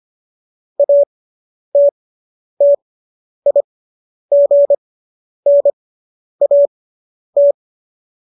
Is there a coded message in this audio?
Yes: Morse code from 0.8 to 7.5 s, heard as a keyed tone, about 25 wpm, spaced as for 10 wpm, 570 Hz. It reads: ATTIGNAT